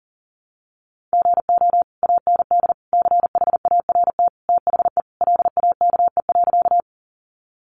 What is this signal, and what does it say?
Morse code, heard as a keyed tone, about 40 wpm, 700 Hz. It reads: GO AND CHART THE LAKE.
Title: Sentences from Top 1300 Words 40wpm